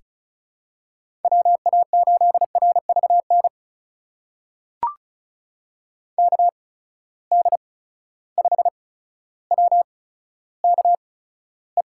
Morse code, heard as a keyed tone, about 34 wpm, spaced as for 15 wpm, 700 Hz. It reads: WA8RVN KD5WKE
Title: USA Call Signs - 6 Characters - 8x Character Spacing - 35wpm